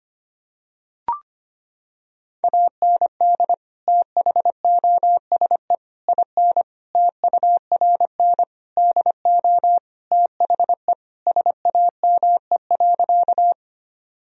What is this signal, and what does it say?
Morse code, heard as a keyed tone, about 25 wpm, 700 Hz. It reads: AND THOSE IN TURN DO THE SAME.
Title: Sentences from Top 1000 Words - Review Part 1 - 25wpm